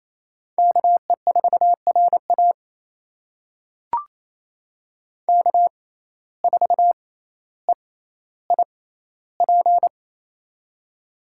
Morse code, heard as a keyed tone, about 27 wpm, spaced as for 14 wpm, 700 Hz. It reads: KE4RA K4EIP
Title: USA Call Signs - 5 Characters - 6x Character Spacing - 28wpm